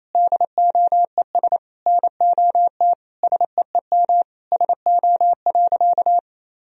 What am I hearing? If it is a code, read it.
Morse code, heard as a keyed tone, about 28 wpm, 700 Hz. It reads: DOES NOT SEEM SO.
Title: Sentences from Top 1000 Words - Review Part 1 - 28wpm